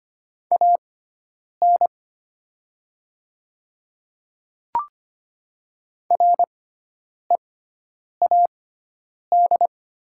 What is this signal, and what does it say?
Morse code, heard as a keyed tone, about 25 wpm, spaced as for 9 wpm, 700 Hz. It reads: AN READ
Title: 6x Character Spacing - Top 500 Words - 25wpm